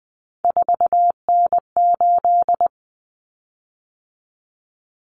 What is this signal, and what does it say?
Morse code, heard as a keyed tone, about 20 wpm, 700 Hz. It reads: 4N8
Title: Common Cut Numbers - 3-Digit 20wpm